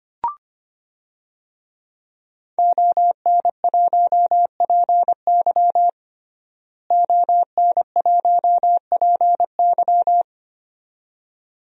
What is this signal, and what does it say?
Morse code, heard as a keyed tone, about 25 wpm, 700 Hz. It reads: ON1PY ON1PY
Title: DXCC-100 Call Signs - 5 Characters - Repeated 2x - 25wpm